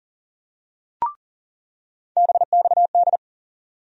Morse code, heard as a keyed tone, about 40 wpm, 700 Hz. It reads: BXD
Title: USA Call Sign Postfixes - 3 Letters 40wpm